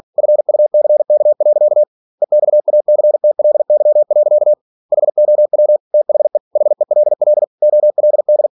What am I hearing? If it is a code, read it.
Morse code, heard as a keyed tone, about 46 wpm, 600 Hz. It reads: FUCK. EXACTLY. HOW THE HELL OLD
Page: Never 47wpm